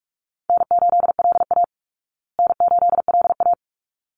Morse code, heard as a keyed tone, about 44 wpm, 700 Hz. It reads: N8LA N8LA
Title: Easy Call Signs Repeated 2x - 45wpm